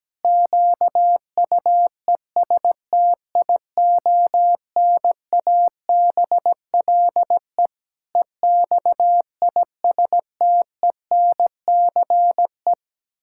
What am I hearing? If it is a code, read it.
Morse code, heard as a keyed tone, about 17 wpm, 700 Hz. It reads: QUESTIONABLE EXISTENCE
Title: Pangram Challenge v2 - 17wpm